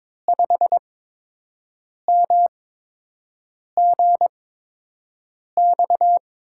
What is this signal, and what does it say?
Morse code, heard as a keyed tone, about 22 wpm, 700 Hz. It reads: 5 M G X
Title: USA Call Signs - 6 Characters - 8x Character Spacing - 22wpm